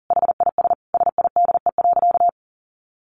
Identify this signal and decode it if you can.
Morse code, heard as a keyed tone, about 40 wpm, 700 Hz. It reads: HIS SIDE.
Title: A Sound Of Thunder 40wpm